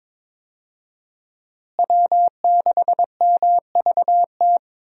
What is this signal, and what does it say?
Morse code, heard as a keyed tone, about 22 wpm, 700 Hz. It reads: W6MVT
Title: USA Call Signs - 5 Characters - Repeated 2x - 22wpm